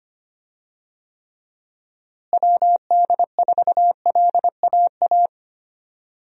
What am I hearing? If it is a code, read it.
Morse code, heard as a keyed tone, about 25 wpm, 700 Hz. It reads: WD4LAA
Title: USA Call Signs - 6 Characters - Repeated 3x - 25wpm